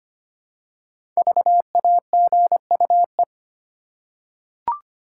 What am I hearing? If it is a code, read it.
Morse code, heard as a keyed tone, about 25 wpm, 700 Hz. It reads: VAGUE
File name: 5 Letter Words - First 3 Letters Given - 25wpm